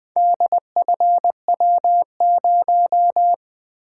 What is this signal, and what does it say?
Morse code, heard as a keyed tone, about 20 wpm, 700 Hz. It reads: DFW0
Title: Four Letter-Number Combinations - 20wpm